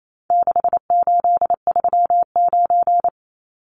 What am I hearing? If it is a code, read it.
Morse code, heard as a keyed tone, about 28 wpm, 700 Hz. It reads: 6839
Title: Numbers - 4-Digit 28wpm